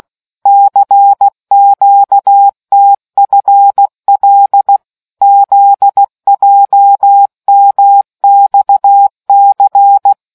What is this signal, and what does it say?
Morse code, heard as a keyed tone, about 16 wpm, 800 Hz. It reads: CQTFL ZJMXC